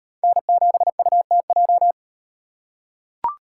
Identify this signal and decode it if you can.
Morse code, heard as a keyed tone, about 37 wpm, 700 Hz. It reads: N7UTJ